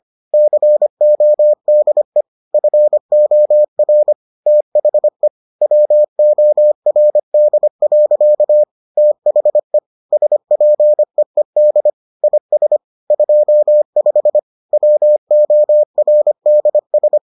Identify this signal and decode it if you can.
Morse code, heard as a keyed tone, about 25 wpm, 600 Hz. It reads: CODE FOR THE WORD. THE SPEED IS 25 WORDS